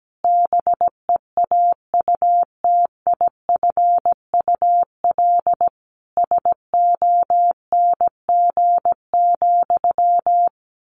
Morse code, heard as a keyed tone, about 17 wpm, 700 Hz. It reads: BEAUTIFUL SONG,
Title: Sentences from Top 1500 Words 17wpm